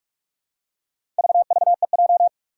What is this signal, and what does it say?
Morse code, heard as a keyed tone, about 44 wpm, 700 Hz. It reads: VVEJ
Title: Four Challenging Letter-Number Combinations 45wpm